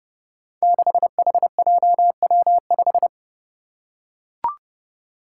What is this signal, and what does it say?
Morse code, heard as a keyed tone, about 30 wpm, 700 Hz. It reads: 6HJW5
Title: Five Challenging Letter-Number Combinations 30wpm